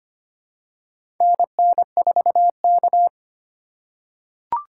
Morse code, heard as a keyed tone, about 25 wpm, 700 Hz. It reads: NN4K
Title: USA Call Signs - 2x Character Spacing - 25wpm